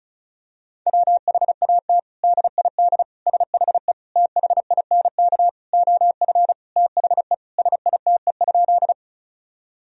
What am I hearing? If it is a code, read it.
Morse code, heard as a keyed tone, about 36 wpm, 700 Hz. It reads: WHAT DID SHE THINK OF THE SITE?
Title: Sentences from Top 1000 Words - Review Part 4 - 35wpm